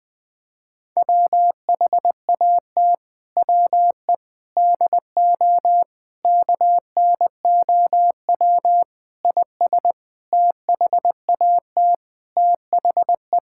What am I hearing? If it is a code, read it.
Morse code, heard as a keyed tone, about 20 wpm, 700 Hz. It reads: WHAT WE DO KNOW IS THAT THE
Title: Sentences with 7-Word Phrases - 20wpm